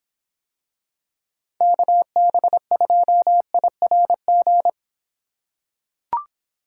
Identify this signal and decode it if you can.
Morse code, heard as a keyed tone, about 26 wpm, 700 Hz. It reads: KB2IRG